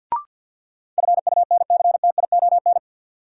Morse code, heard as a keyed tone, about 52 wpm, 700 Hz. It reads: FUNCTION
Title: Sentences from Top 1100 Words - New Word Drill Alt - 50wpm